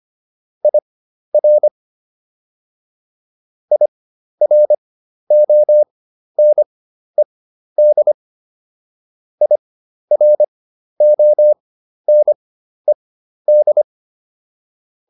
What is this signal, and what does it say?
Morse code, heard as a keyed tone, about 25 wpm, spaced as for 12 wpm, 600 Hz. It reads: IR IRONED IRONED